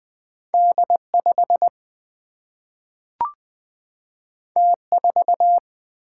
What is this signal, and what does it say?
Morse code, heard as a keyed tone, about 20 wpm, 700 Hz. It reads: D5 T4
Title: Single-Letter Single-Number 20wpm